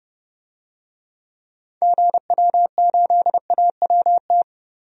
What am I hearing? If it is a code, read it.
Morse code, heard as a keyed tone, about 30 wpm, 700 Hz. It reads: GW8AWT
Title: DXCC-100 Call Signs - 6 Characters - 30wpm